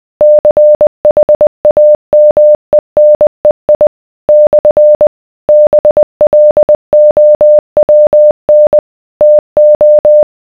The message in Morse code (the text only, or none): CHAMENEI / BLOWN TO